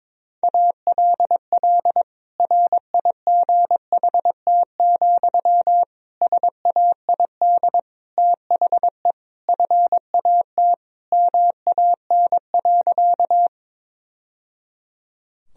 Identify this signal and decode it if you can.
Morse code, heard as a keyed tone, about 21 wpm, 700 Hz. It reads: ALL RIGHT, SAID THE FAT MAN.
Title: Sentences from Top 1000 Words - Review Part 4 - 22wpm